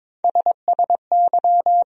Morse code, heard as a keyed tone, about 21 wpm, 700 Hz. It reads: SSY
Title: USA Call Sign Postfixes - 3 Letters 22wpm